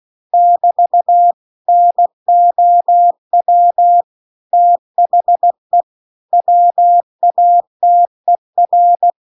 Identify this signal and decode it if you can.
Morse code, heard as a keyed tone, about 16 wpm, 700 Hz. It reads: <BT> NOW THE WATER